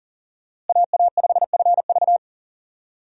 Morse code, heard as a keyed tone, about 40 wpm, 700 Hz. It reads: AA5FV
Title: USA Call Signs - 5 Characters - Repeated 2x - 40wpm